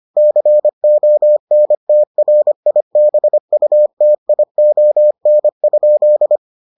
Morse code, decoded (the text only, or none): CONTRIBUTION?